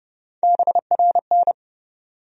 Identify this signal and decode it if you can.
Morse code, heard as a keyed tone, about 30 wpm, 700 Hz. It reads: BRN
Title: USA Call Sign Postfixes - 3 Letters 30wpm